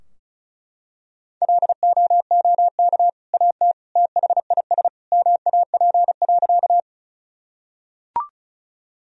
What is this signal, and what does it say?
Morse code, heard as a keyed tone, about 36 wpm, 700 Hz. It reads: LOOK AT THIS MAP.
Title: Sentences from Top 1000 Words - Review Part 2 - 35wpm